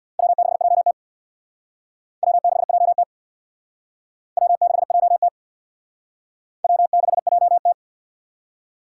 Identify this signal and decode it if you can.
Morse code, heard as a keyed tone, about 60 wpm, 700 Hz. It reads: W6JT W6JT W6JT W6JT
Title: Speed Racing - USA 1x2 Call Signs 50wpm